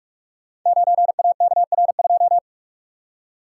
Morse code, heard as a keyed tone, about 44 wpm, 700 Hz. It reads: 9AKR2
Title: Code Groups - 5 Letter:Numbers - 45wpm